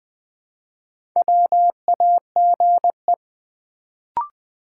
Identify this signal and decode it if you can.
Morse code, heard as a keyed tone, about 20 wpm, 700 Hz. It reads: WAGE
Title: Build A Word - 2 Letters At A Time - 4-Letter Words - 20wpm